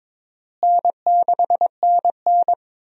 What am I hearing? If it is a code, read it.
Morse code, heard as a keyed tone, about 22 wpm, 700 Hz. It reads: N6NN
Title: USA Call Signs - 4 Characters - Repeated 2x - 22wpm